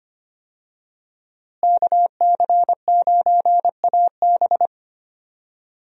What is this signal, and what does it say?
Morse code, heard as a keyed tone, about 25 wpm, 700 Hz. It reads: KC9AB
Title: USA Call Signs - 5 Characters - Repeated 3x - 25wpm